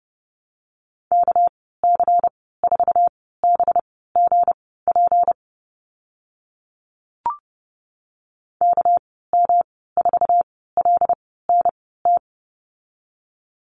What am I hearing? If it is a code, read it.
Morse code, heard as a keyed tone, about 32 wpm, spaced as for 17 wpm, 700 Hz. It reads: KC4BGP KM4LNT
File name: Lesson 37g - Letter x - Call Signs 30wpm